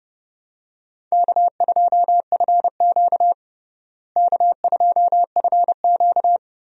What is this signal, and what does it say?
Morse code, heard as a keyed tone, about 30 wpm, 700 Hz. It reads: K2FQ K2FQ
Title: USA Call Signs - 4 Characters - Repeated 3x - 30wpm